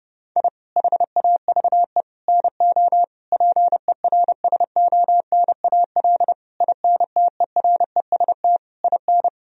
Morse code, heard as a keyed tone, about 30 wpm, 700 Hz. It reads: I HAVE NO PERSONAL INTEREST IN
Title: Sentences from Top 1000 Words - Review Part 4 - 30wpm